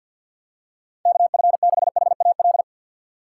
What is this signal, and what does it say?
Morse code, heard as a keyed tone, about 51 wpm, 690 Hz. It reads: KF6HAL
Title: USA Call Signs - 4x Character Spacing - 50wpm